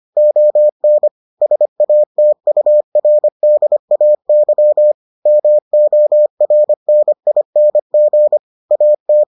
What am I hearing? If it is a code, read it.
Morse code, heard as a keyed tone, about 25 wpm, 600 Hz. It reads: ON SATURDAY MORNING AT